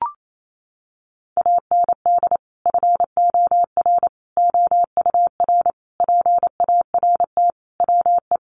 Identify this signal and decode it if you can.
Morse code, heard as a keyed tone, about 28 wpm, 700 Hz. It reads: AND FOR OUR PART WE
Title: Sentences from Top 1100 Words 28wpm